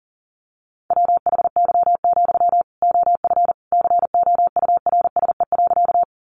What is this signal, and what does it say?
Morse code, heard as a keyed tone, about 40 wpm, 700 Hz. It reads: WHY, OF COURSE.
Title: Sentences from Top 500 Words 40wpm